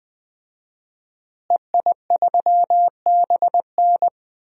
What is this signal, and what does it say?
Morse code, heard as a keyed tone, about 20 wpm, 700 Hz. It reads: EI3BN
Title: DXCC-100 Call Signs - 5 Characters - Repeated 3x - 20wpm